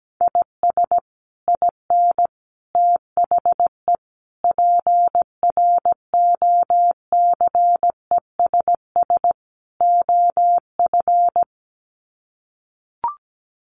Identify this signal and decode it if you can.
Morse code, heard as a keyed tone, about 17 wpm, 700 Hz. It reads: IS IN THE PROCESS OF
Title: Sets of 5 Words Encore2 17wpm